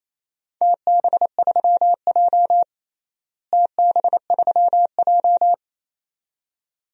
Morse code, heard as a keyed tone, about 28 wpm, 700 Hz. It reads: TB3J TB3J
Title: DXCC-100 Call Signs - 4 Characters - Repeated 2x - 28wpm